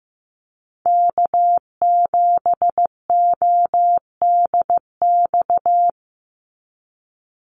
Morse code, heard as a keyed tone, about 15 wpm, 700 Hz. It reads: K7ODX
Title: NA SOTA Chaser Callsigns 15wpm